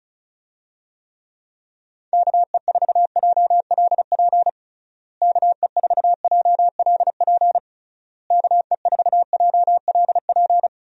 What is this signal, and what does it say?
Morse code, heard as a keyed tone, about 36 wpm, 700 Hz. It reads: KE4JLP KE4JLP KE4JLP
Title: USA Call Signs - 6 Characters - Repeated 3x - 35wpm